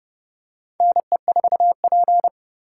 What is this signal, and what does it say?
Morse code, heard as a keyed tone, about 30 wpm, 700 Hz. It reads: NE4P
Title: USA Call Signs - 4 Characters - 4x Character Spacing - 30wpm